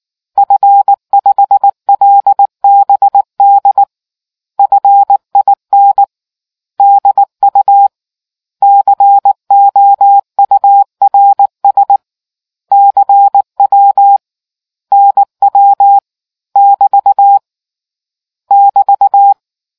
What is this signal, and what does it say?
Morse code, heard as a keyed tone, about 18 wpm, 800 Hz. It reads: F5LBD FIN DU COURS CW NW <BT> <BT>